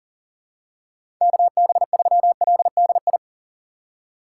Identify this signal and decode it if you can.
Morse code, heard as a keyed tone, about 38 wpm, 700 Hz. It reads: KB3LDI
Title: USA Call Signs - 6 Characters - 3x Character Spacing - 40wpm